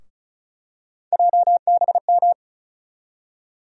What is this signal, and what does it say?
Morse code, heard as a keyed tone, about 36 wpm, 700 Hz. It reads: JBM
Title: USA Call Sign Postfixes - 3 Letters 35wpm